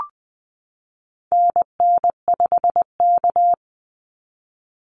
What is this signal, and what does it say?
Morse code, heard as a keyed tone, about 19 wpm, 700 Hz. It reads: NN5K
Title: NA SOTA Activator Callsigns 20wpm